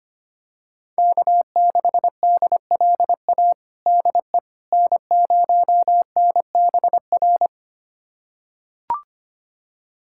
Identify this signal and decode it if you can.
Morse code, heard as a keyed tone, about 25 wpm, 700 Hz. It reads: K6DLA DE N0NBR
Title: USA Callsign DE Callsign - 5 Characters - 25wpm